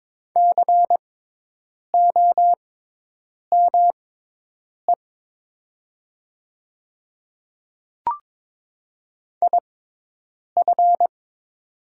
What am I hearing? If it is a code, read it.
Morse code, heard as a keyed tone, about 22 wpm, spaced as for 8 wpm, 700 Hz. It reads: COME IF
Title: 6x Character Spacing - Top 200 Words - 22wpm